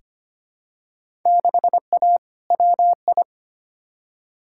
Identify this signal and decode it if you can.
Morse code, heard as a keyed tone, about 25 wpm, 700 Hz. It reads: 6A WI
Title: Field Day Exchanges - 25wpm